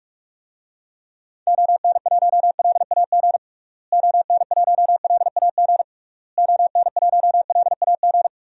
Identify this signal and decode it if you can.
Morse code, heard as a keyed tone, about 46 wpm, 700 Hz. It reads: ON1LAG ON1LAG ON1LAG
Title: DXCC-100 Call Signs - 6 Characters - Repeated 3x - 45wpm